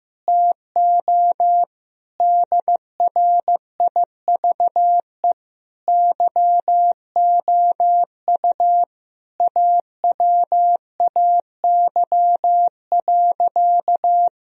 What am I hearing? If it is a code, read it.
Morse code, heard as a keyed tone, about 15 wpm, 700 Hz. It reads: TO DRIVE YOU AWAY.
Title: Sentences from Top 1000 Words - Review Part 2 - 15wpm